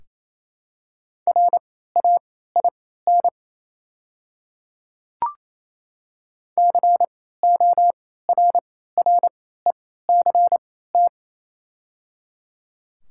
Morse code, heard as a keyed tone, about 28 wpm, spaced as for 16 wpm, 700 Hz. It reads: RAIN CORRECT
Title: 3x Character Spacing - Top 500 Words - 28wpm